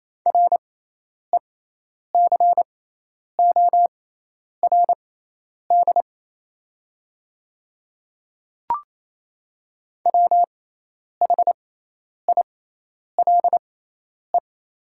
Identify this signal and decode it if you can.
Morse code, heard as a keyed tone, about 28 wpm, spaced as for 10 wpm, 700 Hz. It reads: RECORD WHILE
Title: 6x Character Spacing - Top 500 Words - 28wpm